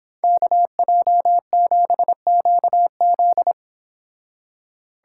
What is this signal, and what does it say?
Morse code, heard as a keyed tone, about 30 wpm, 700 Hz. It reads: KJ7QZ